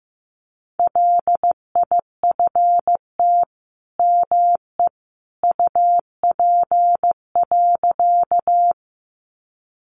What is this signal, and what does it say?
Morse code, heard as a keyed tone, about 15 wpm, 700 Hz. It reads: LIFT ME UP.
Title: Sentences from Top 1000 Words 15wpm